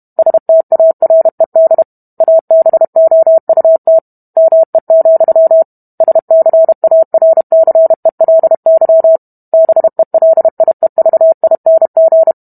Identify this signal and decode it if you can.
Morse code, heard as a keyed tone, about 32 wpm, 650 Hz. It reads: STARED ABOUT ME, SCARCELY BELIEVING